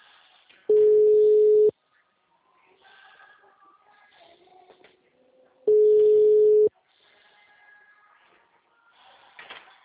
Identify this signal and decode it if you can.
Ringback tone (Europe, 425 Hz).
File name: out- -2016- - - .47801